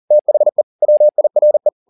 Morse code, heard as a keyed tone, about 40 wpm, 600 Hz. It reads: THE WIRE
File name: the old man and the sea 2 40wpm